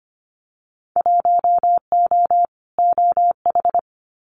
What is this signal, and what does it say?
Morse code, heard as a keyed tone, about 25 wpm, 700 Hz. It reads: 1O OH
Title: Winter Field Day Exchanges 25wpm